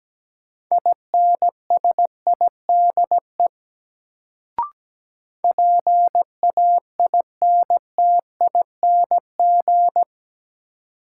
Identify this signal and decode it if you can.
Morse code, heard as a keyed tone, about 17 wpm, 700 Hz. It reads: INSIDE PAINTING
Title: Sentences from Top 1200 Words - New Word Drill Alt - 17wpm